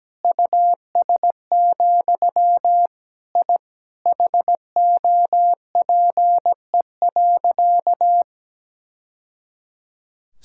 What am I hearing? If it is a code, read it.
Morse code, heard as a keyed tone, about 17 wpm, 700 Hz. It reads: US, I HOPE.
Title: Sentences from Top 1000 Words - Review Part 6 - 17wpm